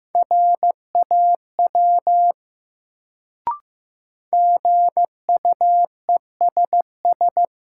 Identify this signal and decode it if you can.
Morse code, heard as a keyed tone, about 15 wpm, 700 Hz. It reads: RAW GUESS
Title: Sentences from Top 1500 Words - New Word Drill Alt - 15wpm